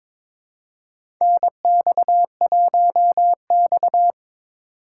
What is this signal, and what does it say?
Morse code, heard as a keyed tone, about 21 wpm, 700 Hz. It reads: NX1X
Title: USA Call Signs - 4 Characters - 2x Character Spacing - 22wpm